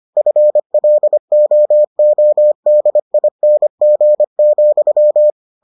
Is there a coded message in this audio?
Yes: Morse code from 0.2 to 5.3 s, heard as a keyed tone, about 25 wpm, 600 Hz. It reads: FLOODING,